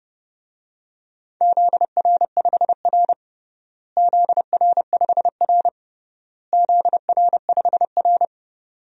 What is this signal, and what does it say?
Morse code, heard as a keyed tone, about 30 wpm, 700 Hz. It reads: ZR5R ZR5R ZR5R
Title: DXCC-100 Call Signs - 4 Characters - Repeated 3x - 30wpm